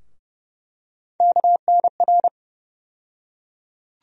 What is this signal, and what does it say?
Morse code, heard as a keyed tone, about 30 wpm, 700 Hz. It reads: KNR
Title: USA Call Sign Postfixes - 3 Letters 30wpm